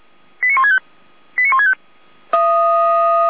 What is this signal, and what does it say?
Five-tone selective call (ZVEI), heard as quick runs of five tones, tones of about 70 ms. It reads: 89165 89165
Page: SDR/ZVEI